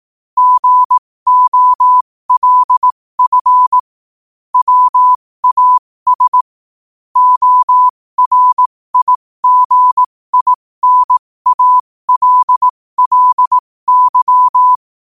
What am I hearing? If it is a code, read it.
Morse code, heard as a keyed tone, about 18 wpm, spaced as for 15 wpm, 1000 Hz. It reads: GOLF WAS ORIGINALLY
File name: qotdfiles / 15wpm